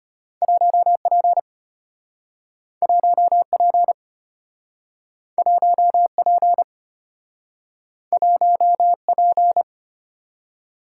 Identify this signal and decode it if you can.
Morse code, heard as a keyed tone, about 35 wpm, 700 Hz. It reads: 1P 1P 1P 1P